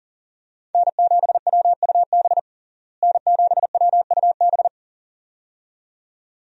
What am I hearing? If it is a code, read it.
Morse code, heard as a keyed tone, about 40 wpm, 700 Hz. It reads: N7WUB N7WUB
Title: USA Call Signs - 5 Characters - Repeated 2x - 40wpm